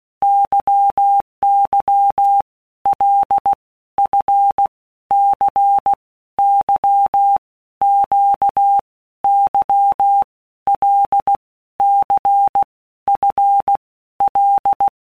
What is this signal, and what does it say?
Morse code, heard as a keyed tone, about 16 wpm, 800 Hz. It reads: YY L F C Y Q Y L C F L
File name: Basic Letter Groups By Blocks / Groups@15 / LFCYQ@15